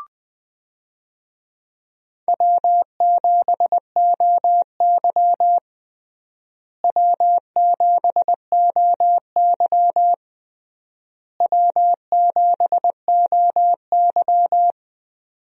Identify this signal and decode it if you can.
Morse code, heard as a keyed tone, about 21 wpm, 700 Hz. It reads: W7OY W7OY W7OY
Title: USA Call Signs - 4 Characters - Repeated 3x - 20wpm